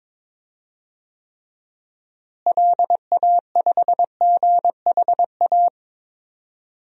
Morse code, heard as a keyed tone, about 21 wpm, 700 Hz. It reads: LA5GHA
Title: DXCC-100 Call Signs - 6 Characters - Repeated 3x - 22wpm